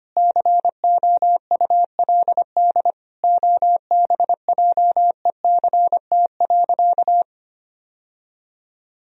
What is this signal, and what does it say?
Morse code, heard as a keyed tone, about 25 wpm, 700 Hz. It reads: COULD OBJECT.
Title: Sentences from Top 1000 Words - Review Part 3 - 25wpm